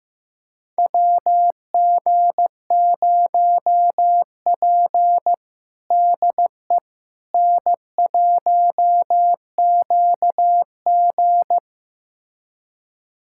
Morse code, heard as a keyed tone, about 15 wpm, 700 Hz. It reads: WG0P DE N1QG
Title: USA Callsign DE Callsign - 4 Characters - 15wpm